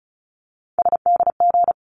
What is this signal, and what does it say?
Morse code, heard as a keyed tone, about 34 wpm, 700 Hz. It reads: SDG
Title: ARRL Sections 35wpm